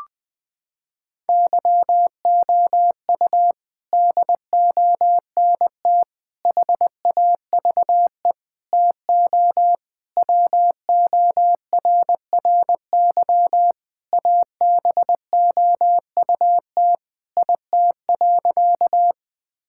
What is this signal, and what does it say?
Morse code, heard as a keyed tone, about 21 wpm, 700 Hz. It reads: YOU DONT HAVE TO WORRY ABOUT IT.
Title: Sentences with 6-Word Phrases - 20wpm